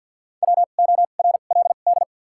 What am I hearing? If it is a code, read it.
Morse code, heard as a keyed tone, about 53 wpm, spaced as for 37 wpm, 700 Hz. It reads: WORLD
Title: 2x Character Spacing - Top 200 Words - 50wpm